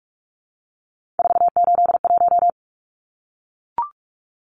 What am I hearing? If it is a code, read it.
Morse code, heard as a keyed tone, about 44 wpm, 700 Hz. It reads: 481
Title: Numbers - 3-Digit 45wpm